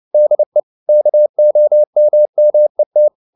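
Morse code, heard as a keyed tone, about 29 wpm, 600 Hz. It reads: DE KOMMET
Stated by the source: Fast (QRQ) Morse at 29 wpm, from CW med Gnister.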